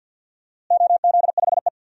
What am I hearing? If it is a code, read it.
Morse code, heard as a keyed tone, about 51 wpm, 700 Hz. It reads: OZ5E
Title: DXCC-100 Call Signs - 4 Characters - Repeated 3x - 50wpm